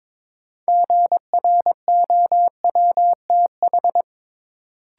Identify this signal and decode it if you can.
Morse code, heard as a keyed tone, about 21 wpm, 700 Hz. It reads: GROWTH
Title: Sentences from Top 1000 Words - Review Part 5 - New Word Drill - 22wpm